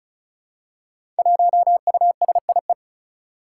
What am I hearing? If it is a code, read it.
Morse code, heard as a keyed tone, about 36 wpm, 700 Hz. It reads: 1USIE
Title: Five Challenging Letter-Number Combinations 35wpm